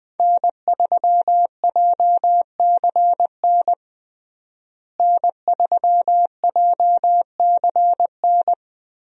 Morse code, heard as a keyed tone, about 20 wpm, 700 Hz. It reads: N3JCN N3JCN
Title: USA Call Signs - 5 Characters - Repeated 2x - 20wpm